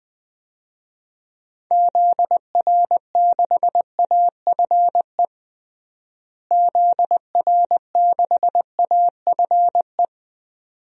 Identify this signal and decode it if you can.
Morse code, heard as a keyed tone, about 21 wpm, 700 Hz. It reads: ZR6AFE ZR6AFE
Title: DXCC-100 Call Signs - 6 Characters - Repeated 2x - 20wpm